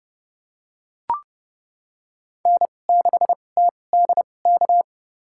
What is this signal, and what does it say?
Morse code, heard as a keyed tone, about 30 wpm, spaced as for 22 wpm, 700 Hz. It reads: N6TDK
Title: USA Call Signs - 5 Characters - 2x Character Spacing - 30wpm